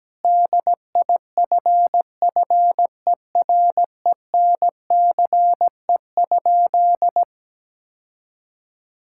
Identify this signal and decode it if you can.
Morse code, heard as a keyed tone, about 17 wpm, 700 Hz. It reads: DIFFERENCE?
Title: Sentences from Top 1000 Words - Review Part 5 - 17wpm